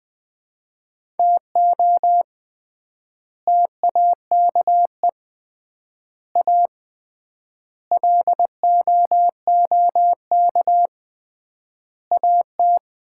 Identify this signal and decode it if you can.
Morse code, heard as a keyed tone, about 20 wpm, 700 Hz. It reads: TO TAKE A LOOK AT
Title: 3x Word-Spacing - Sets of 5 Words - 20wpm